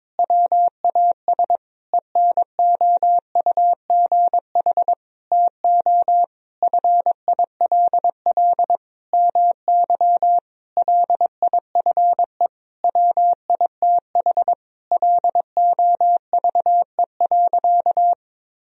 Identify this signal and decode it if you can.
Morse code, heard as a keyed tone, about 22 wpm, 700 Hz. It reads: WAS ENOUGH TO FILL MY LIFE WITH LOVE.
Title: Sentences from Top 1000 Words - Review Part 3 - 22wpm